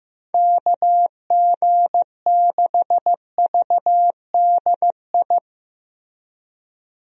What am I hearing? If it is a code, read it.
Morse code, heard as a keyed tone, about 15 wpm, 700 Hz. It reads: KG6VDI
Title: USA Call Signs - 6 Characters - Repeated 3x - 15wpm